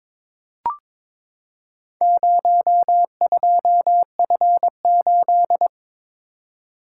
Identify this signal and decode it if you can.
Morse code, heard as a keyed tone, about 23 wpm, 700 Hz. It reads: 02F8
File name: practice / 22wpm / Four Letter-Number Combinations - 22wpm